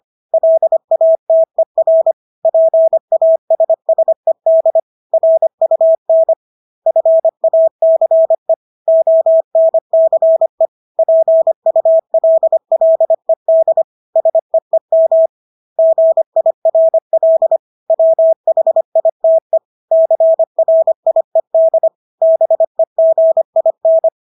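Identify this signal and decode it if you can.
Morse code, heard as a keyed tone, about 25 wpm, 650 Hz. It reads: LATER PASSED RUN FACE ONCE PULLED SEEM GIRL WHITE CRIED BEGIN
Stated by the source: Morse practice of English words at 25 wpm from CW med Gnister.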